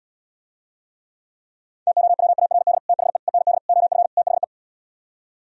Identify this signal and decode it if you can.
Morse code, heard as a keyed tone, about 77 wpm, 700 Hz. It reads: TOWARD THE END OF THE